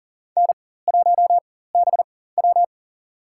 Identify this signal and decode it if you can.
Morse code, heard as a keyed tone, about 38 wpm, spaced as for 18 wpm, 700 Hz. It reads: N1BW
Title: USA Call Signs - 4 Characters - 4x Character Spacing - 40wpm